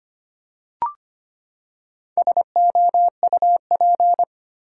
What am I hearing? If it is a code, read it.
Morse code, heard as a keyed tone, about 25 wpm, 700 Hz. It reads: SOUP
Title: Four Letter Words - 25wpm